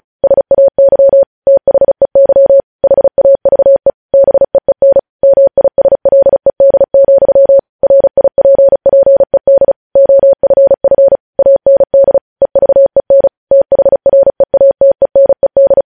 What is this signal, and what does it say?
Morse code, heard as a keyed tone, about 36 wpm, 550 Hz. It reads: SAY THEY HAVE BEEN MISLED, RIPPED OFF AND EVEN THREATENED
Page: news headlines in morse code at 35 wpm